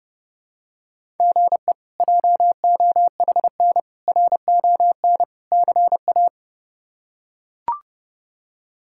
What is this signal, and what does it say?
Morse code, heard as a keyed tone, about 30 wpm, 700 Hz. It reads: GE JOHN RON CA
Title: K1USN SST QSO Practice - 30wpm